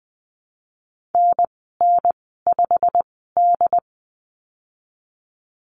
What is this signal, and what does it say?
Morse code, heard as a keyed tone, about 20 wpm, spaced as for 15 wpm, 700 Hz. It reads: NN5D
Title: 147 Lesson 24g - Letter y - Call Signs 20wpm